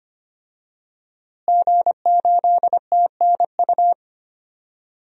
Morse code, heard as a keyed tone, about 25 wpm, 700 Hz. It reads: G8TNU
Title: DXCC-100 Call Signs - 5 Characters - Repeated 3x - 25wpm